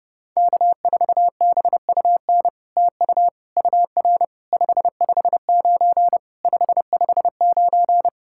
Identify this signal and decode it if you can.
Morse code, heard as a keyed tone, about 30 wpm, 700 Hz. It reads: K4BUN TU UR 559 559
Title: SOTA QSO Practice 30wpm